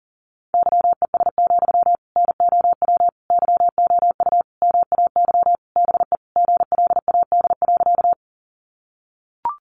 Morse code, heard as a keyed tone, about 40 wpm, 700 Hz. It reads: YES, NOW YOU MAY BE GLAD.
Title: Sentences from Top 800 Words 40wpm